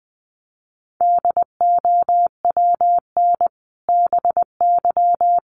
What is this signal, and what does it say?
Morse code, heard as a keyed tone, about 20 wpm, 700 Hz. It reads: DOWN BY
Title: Sentences from Top 1200 Words 20wpm